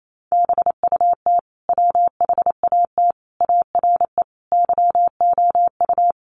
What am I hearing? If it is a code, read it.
Morse code, heard as a keyed tone, about 28 wpm, 700 Hz. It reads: BUT WHAT ARE YOU
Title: Sentences from Top 100 Words 28wpm